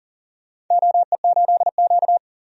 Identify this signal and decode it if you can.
Morse code, heard as a keyed tone, about 40 wpm, 700 Hz. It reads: OE8Q
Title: DXCC-100 Call Signs - 4 Characters - 40wpm